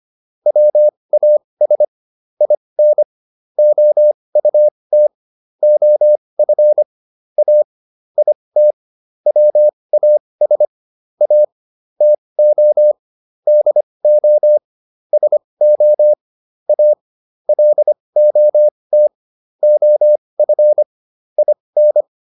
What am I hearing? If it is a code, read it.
Morse code, heard as a keyed tone, about 25 wpm, spaced as for 20 wpm, 600 Hz. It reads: WAS IN OUT OF A IT WAS A TO DO SO A LOT OF IN